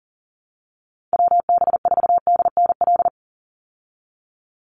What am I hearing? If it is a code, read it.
Morse code, heard as a keyed tone, about 40 wpm, 700 Hz. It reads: WB4DNL
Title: Call Signs Repeated 2x - 40wpm